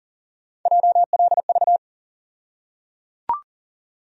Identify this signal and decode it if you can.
Morse code, heard as a keyed tone, about 37 wpm, 700 Hz. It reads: JLV
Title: USA Call Sign Postfixes - 3 Letters 40wpm